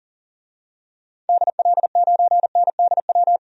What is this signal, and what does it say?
Morse code, heard as a keyed tone, about 40 wpm, 700 Hz. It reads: DL9NDW
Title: DXCC-100 Call Signs - 6 Characters - 40wpm